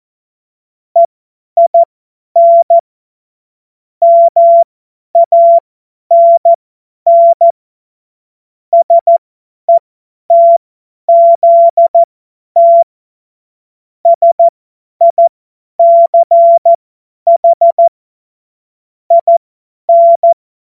Morse code, heard as a keyed tone, about 14 wpm, spaced as for 10 wpm, 680 Hz. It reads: EIN MANN SETZT SICH IN
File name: Morsekurs / Witze / Witz 07. September 2021, 50 F70